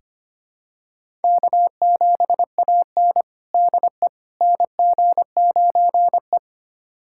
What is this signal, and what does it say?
Morse code, heard as a keyed tone, about 25 wpm, 700 Hz. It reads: K7AN DE NG9E
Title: USA Callsign DE Callsign - 4 Characters - 25wpm